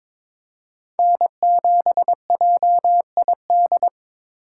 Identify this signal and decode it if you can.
Morse code, heard as a keyed tone, about 22 wpm, 700 Hz. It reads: N7JID
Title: USA Call Signs - 5 Characters - 3x Character Spacing - 22wpm